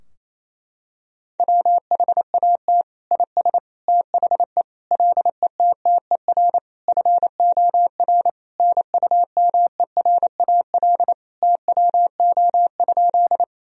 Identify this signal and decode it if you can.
Morse code, heard as a keyed tone, about 27 wpm, 700 Hz. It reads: WHAT IS THE LETTER FOR NUMERAL TWO?
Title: Sentences from Top 1000 Words - Review Part 2 - 28wpm